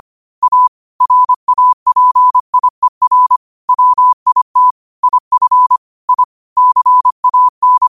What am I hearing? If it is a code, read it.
Morse code, heard as a keyed tone, about 25 wpm, 1000 Hz. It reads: A RAPIER WIT IF I CAN
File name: qotdfiles / 25wpm